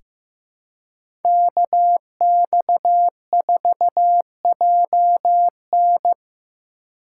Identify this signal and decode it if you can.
Morse code, heard as a keyed tone, about 15 wpm, 700 Hz. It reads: KX4JN
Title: USA Call Signs - 5 Characters - 3x Character Spacing - 15wpm